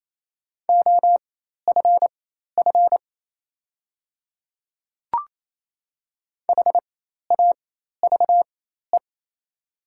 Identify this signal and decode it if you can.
Morse code, heard as a keyed tone, about 28 wpm, spaced as for 13 wpm, 700 Hz. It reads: OFF HAVE
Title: 4x Character Spacing - Top 200 Words - 28wpm